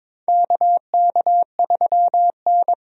Morse code, heard as a keyed tone, about 22 wpm, 700 Hz. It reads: KK3N
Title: USA Call Signs - 4 Characters - 22wpm